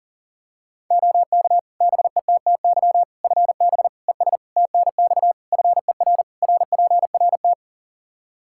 Morse code, heard as a keyed tone, about 40 wpm, 700 Hz. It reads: OK BETTY FB ES TNX FER RPRT
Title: Practice QSO - Exchange 2 - Info - 40wpm